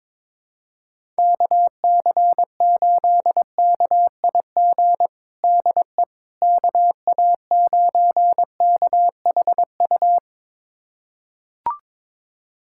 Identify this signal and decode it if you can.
Morse code, heard as a keyed tone, about 22 wpm, 700 Hz. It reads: KC8KIG DE KA9KHU
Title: USA Callsign DE Callsign - 6 Characters - 22wpm